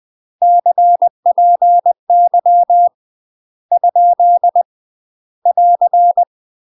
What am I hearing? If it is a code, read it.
Morse code, heard as a keyed tone, about 20 wpm, 700 Hz. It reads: CPY ? <AR>
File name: UK MASTER QSOs TRAINING / 20WPM / UK MASTER QSOs 0047